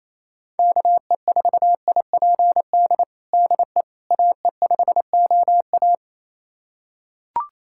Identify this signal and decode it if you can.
Morse code, heard as a keyed tone, about 28 wpm, 700 Hz. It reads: KE4IPD DE AE5OA
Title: USA Callsign DE Callsign 28wpm